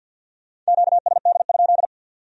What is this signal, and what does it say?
Morse code, heard as a keyed tone, about 63 wpm, 700 Hz. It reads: QSD?